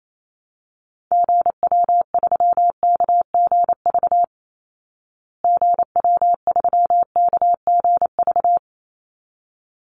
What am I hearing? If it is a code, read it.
Morse code, heard as a keyed tone, about 28 wpm, 700 Hz. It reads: GW3KGV GW3KGV
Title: Call Signs Repeated 3x - 28wpm